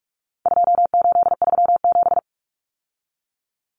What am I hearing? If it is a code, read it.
Morse code, heard as a keyed tone, about 44 wpm, 700 Hz. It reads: 2837
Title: Common Cut Numbers - 4-Digit 45wpm